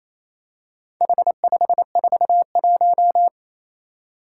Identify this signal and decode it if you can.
Morse code, heard as a keyed tone, about 28 wpm, 700 Hz. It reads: H541
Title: Four Challenging Letter-Number Combinations 28wpm